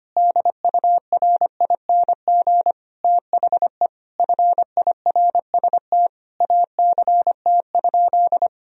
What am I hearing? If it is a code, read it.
Morse code, heard as a keyed tone, about 25 wpm, 700 Hz. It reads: DURING THE FIRST ACT?
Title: Sentences from Top 1000 Words - Review Part 2 - 25wpm